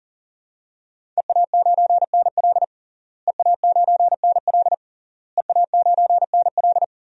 Morse code, heard as a keyed tone, about 40 wpm, 700 Hz. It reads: EA9NL EA9NL EA9NL
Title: DXCC-100 Call Signs - 5 Characters - Repeated 3x - 40wpm